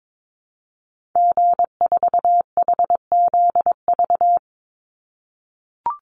Morse code, heard as a keyed tone, about 22 wpm, 700 Hz. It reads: G4HZV
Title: Call Signs - 22wpm